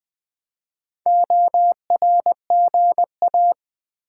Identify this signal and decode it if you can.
Morse code, heard as a keyed tone, about 20 wpm, 700 Hz. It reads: ORGA
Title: Build A Word - 4 Letters At A Time - 12-Letter Words - 20wpm